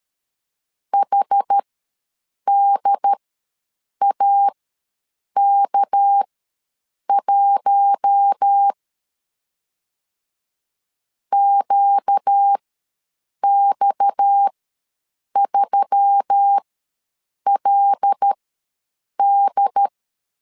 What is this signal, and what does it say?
Morse code, heard as a keyed tone, about 13 wpm, spaced as for 7 wpm, 800 Hz. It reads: HDAK1 QX3LD
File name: LEZIONI / 6 PKX0 / 6 PKX ws 12wpm 3106